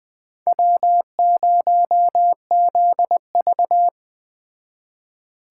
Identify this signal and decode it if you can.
Morse code, heard as a keyed tone, about 21 wpm, 700 Hz. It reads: W0ZV
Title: Speed Racing - USA 1x2 Call Signs 17wpm